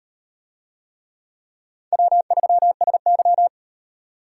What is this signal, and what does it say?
Morse code, heard as a keyed tone, about 36 wpm, 700 Hz. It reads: W3SY